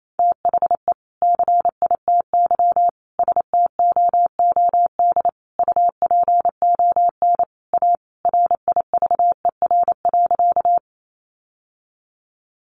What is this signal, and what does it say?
Morse code, heard as a keyed tone, about 28 wpm, 700 Hz. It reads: THE CITY STOOD UPON A RIVER.
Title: Sentences from Top 1100 Words 28wpm